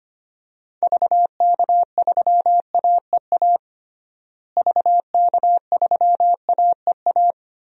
Morse code, heard as a keyed tone, about 25 wpm, 700 Hz. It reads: VK3AEA VK3AEA
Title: DXCC-100 Call Signs - 6 Characters - Repeated 2x - 25wpm